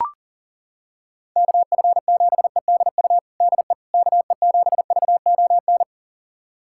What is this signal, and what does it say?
Morse code, heard as a keyed tone, about 40 wpm, 700 Hz. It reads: KF7EDU DE KE7VON
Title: USA Callsign DE Callsign - 6 Characters - 40wpm